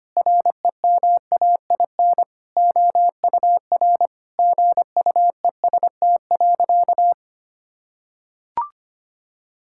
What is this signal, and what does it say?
Morse code, heard as a keyed tone, about 25 wpm, 700 Hz. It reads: REMAIN OUR GUEST.
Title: Sentences from Top 1600 Words 25wpm